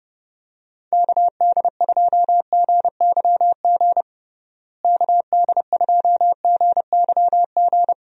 Morse code, heard as a keyed tone, about 32 wpm, 700 Hz. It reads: KD2GYG KD2GYG
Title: USA Call Signs - 6 Characters - Repeated 2x - 30wpm